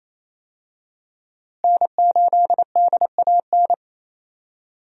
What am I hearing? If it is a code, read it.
Morse code, heard as a keyed tone, about 28 wpm, 700 Hz. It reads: N8DAN
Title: USA Call Signs - 5 Characters - Repeated 2x - 28wpm